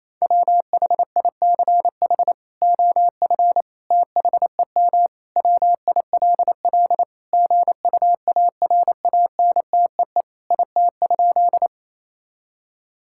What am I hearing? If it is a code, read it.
Morse code, heard as a keyed tone, about 28 wpm, 700 Hz. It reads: WHICH OF THEM WILL GUARANTEE IT?
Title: Sentences from Top 1600 Words 28wpm